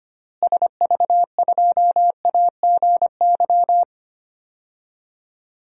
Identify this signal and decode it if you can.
Morse code, heard as a keyed tone, about 25 wpm, 700 Hz. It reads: SV2AGY
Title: DXCC-100 Call Signs - 6 Characters - Repeated 2x - 25wpm